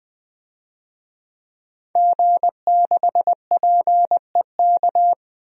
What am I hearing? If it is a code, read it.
Morse code, heard as a keyed tone, about 20 wpm, 700 Hz. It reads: G6PEK
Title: DXCC-100 Call Signs - 5 Characters - 20wpm